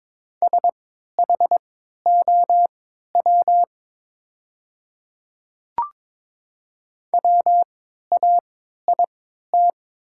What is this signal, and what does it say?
Morse code, heard as a keyed tone, about 22 wpm, spaced as for 12 wpm, 700 Hz. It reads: SHOW WAIT
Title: 3x Character Spacing - Top 500 Words - 22wpm